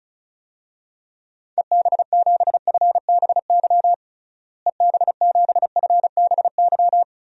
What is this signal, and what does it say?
Morse code, heard as a keyed tone, about 36 wpm, 700 Hz. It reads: EB7FBY EB7FBY
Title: DXCC-100 Call Signs - 6 Characters - Repeated 3x - 35wpm